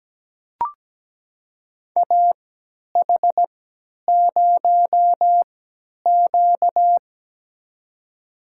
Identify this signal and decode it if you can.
Morse code, heard as a keyed tone, about 17 wpm, spaced as for 10 wpm, 700 Hz. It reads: AH0Q
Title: USA Call Signs - 4 Characters - 3x Character Spacing - 17wpm